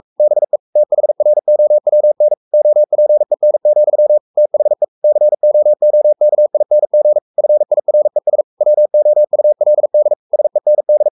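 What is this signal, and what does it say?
Morse code, heard as a keyed tone, about 44 wpm, 600 Hz. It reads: BE THROWN OPEN, THE COOKING FIRES WOULD SEND
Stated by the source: Morse code at 43 wpm sending Ken Follett's novel Never in English, from CW med Gnister.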